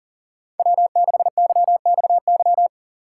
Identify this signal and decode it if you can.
Morse code, heard as a keyed tone, about 40 wpm, 700 Hz. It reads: W6YXY
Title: USA Call Signs - 2x Character Spacing - 40wpm